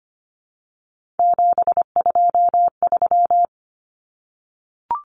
Morse code, heard as a keyed tone, about 25 wpm, 700 Hz. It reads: 723
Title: Common Cut Numbers - 3-Digit 25wpm